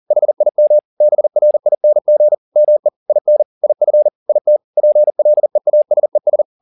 Morse code, heard as a keyed tone, about 40 wpm, 600 Hz. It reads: HIM BRING ME IN IF IT PLEASES
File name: the old man and the sea 2 40wpm